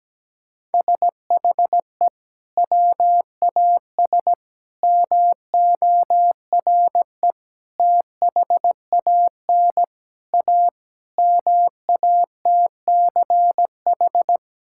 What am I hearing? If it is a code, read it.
Morse code, heard as a keyed tone, about 17 wpm, 700 Hz. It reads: SHE WAS MORE THAN A MATCH
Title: Sentences from Top 1600 Words 17wpm